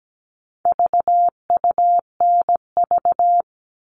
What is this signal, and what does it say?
Morse code, heard as a keyed tone, about 17 wpm, 700 Hz. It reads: VUNV
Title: All Cut Numbers - 4-Digit 17wpm